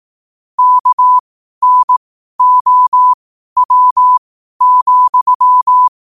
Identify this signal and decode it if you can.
Morse code, heard as a keyed tone, about 18 wpm, spaced as for 12 wpm, 1000 Hz. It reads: KNOW,
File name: qotdfiles / 12wpm